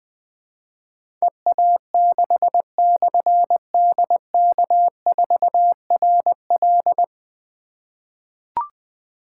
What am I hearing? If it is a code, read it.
Morse code, heard as a keyed tone, about 20 wpm, 700 Hz. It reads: EA6/DK4RL
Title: Difficult Call Signs 20wpm